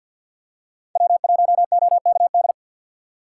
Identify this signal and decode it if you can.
Morse code, heard as a keyed tone, about 49 wpm, 700 Hz. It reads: W1OKD
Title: USA Call Signs - 5 Characters - 6x Character Spacing - 50wpm